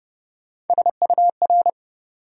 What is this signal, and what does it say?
Morse code, heard as a keyed tone, about 29 wpm, 700 Hz. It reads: SUR
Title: USA Call Sign Postfixes - 3 Letters 30wpm